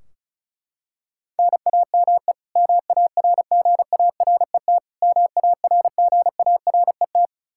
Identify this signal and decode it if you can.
Morse code, heard as a keyed tone, about 36 wpm, 700 Hz. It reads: NAME MARGARET MARGARET
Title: K1USN SST QSO Practice 35wpm